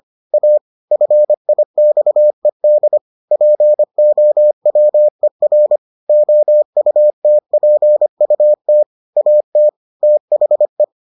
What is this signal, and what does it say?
Morse code, heard as a keyed tone, about 25 wpm, 600 Hz. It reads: A FIXED POWER OUTPUT AT THE